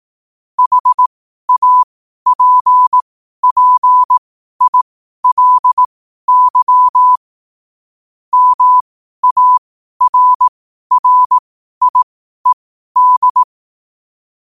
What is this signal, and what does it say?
Morse code, heard as a keyed tone, about 18 wpm, spaced as for 12 wpm, 1000 Hz. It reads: HAPPILY MARRIED
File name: qotdfiles / 12wpm